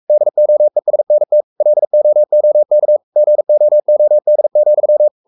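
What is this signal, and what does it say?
Morse code, heard as a keyed tone, about 44 wpm, 600 Hz. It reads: DOESNT LOOK GOOD,
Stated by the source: Morse code at 43 wpm sending Ken Follett's novel Never in English, from CW med Gnister.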